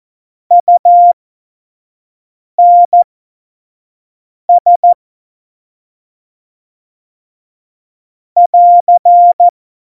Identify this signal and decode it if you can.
Morse code, heard as a keyed tone, about 14 wpm, spaced as for 5 wpm, 700 Hz. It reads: UNS <AR>
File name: Morsekurs / Witze / Witz 08. Dezember2020, 25 F70